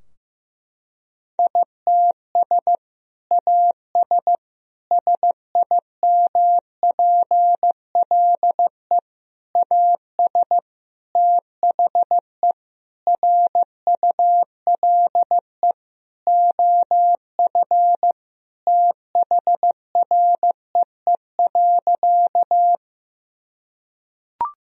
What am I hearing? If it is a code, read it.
Morse code, heard as a keyed tone, about 15 wpm, 700 Hz. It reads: ITS AS SIMPLE AS THE RULE OF THREE.
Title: Sentences from Top 1000 Words - Review Part 2 - 15wpm